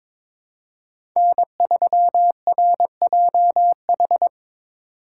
Morse code, heard as a keyed tone, about 21 wpm, 700 Hz. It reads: N3RJH
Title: USA Call Signs - 3x Character Spacing - 22wpm